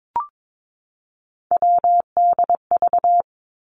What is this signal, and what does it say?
Morse code, heard as a keyed tone, about 22 wpm, 700 Hz. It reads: WDV
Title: Three Challenging Letter-Number Combinations 22wpm